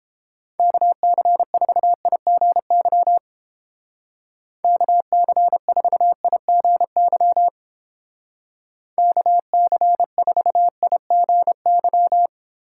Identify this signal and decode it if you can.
Morse code, heard as a keyed tone, about 29 wpm, 700 Hz. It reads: KC4IGY KC4IGY KC4IGY